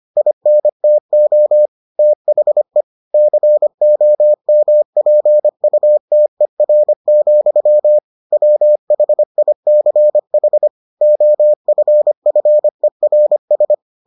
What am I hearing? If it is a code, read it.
Morse code, heard as a keyed tone, about 25 wpm, 600 Hz. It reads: INTO THE COMPUTER, WHICH OFFERS